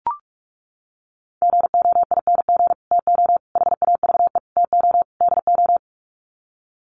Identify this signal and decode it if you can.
Morse code, heard as a keyed tone, about 44 wpm, 700 Hz. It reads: GOING TO HAVE TO DO
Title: Sets of 5 Words Encore 45wpm